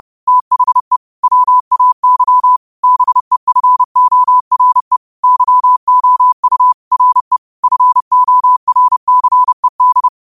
Morse code, heard as a keyed tone, about 30 wpm, 1000 Hz. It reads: THE WAY BEFORE YOU RE FORCED
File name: qotdfiles / 30wpm